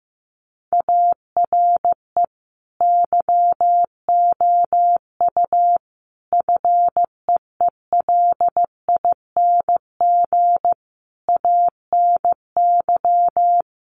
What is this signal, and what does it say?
Morse code, heard as a keyed tone, about 15 wpm, 700 Hz. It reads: ARE YOU FEELING ANY
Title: Sentences from Top 1200 Words 15wpm